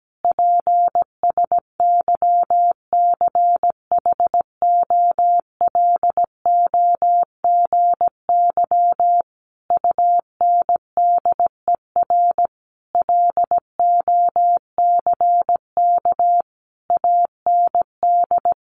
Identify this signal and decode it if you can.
Morse code, heard as a keyed tone, about 17 wpm, 700 Hz. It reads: PSYCHOLOGY UNDER LOCK AND
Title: Sentences from Top 1400 Words 17wpm